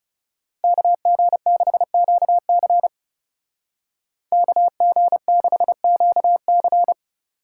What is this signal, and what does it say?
Morse code, heard as a keyed tone, about 34 wpm, 700 Hz. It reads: KG6QC KG6QC
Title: Speed Racing - USA 2x2 Call Signs 30wpm